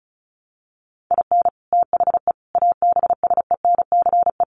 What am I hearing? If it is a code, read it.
Morse code, heard as a keyed tone, about 34 wpm, 700 Hz. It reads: IN THE ABSENCE
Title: Sets of 4 Words 35wpm